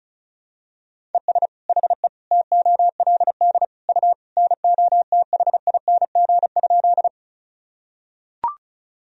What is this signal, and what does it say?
Morse code, heard as a keyed tone, about 36 wpm, 700 Hz. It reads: ES HE TOLD U NOTHING?
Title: Sentences from Top 500 Words and Abbreviations 35wpm